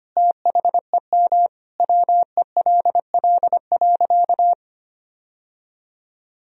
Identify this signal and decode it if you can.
Morse code, heard as a keyed tone, about 25 wpm, 700 Hz. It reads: THEM WELL.
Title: Sentences from Top 1000 Words - Review Part 1 - 25wpm